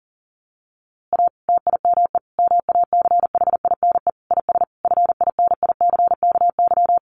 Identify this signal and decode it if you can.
Morse code, heard as a keyed tone, about 40 wpm, 700 Hz. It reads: A TIME MACHINE IS FINICKY
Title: A Sound Of Thunder 40wpm